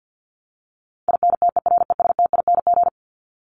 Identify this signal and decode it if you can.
Morse code, heard as a keyed tone, about 50 wpm, 700 Hz. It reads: INTERESTING
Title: ING Suffix - Top 100 Words with Base Words 50wpm